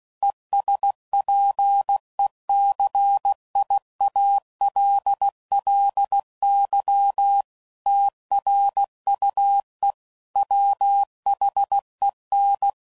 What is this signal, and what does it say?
Morse code, heard as a keyed tone, about 16 wpm, 800 Hz. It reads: ESPECIALLY TRUE WHEN
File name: CW Code Groups @15WPM / Technical Subjects / AND Gate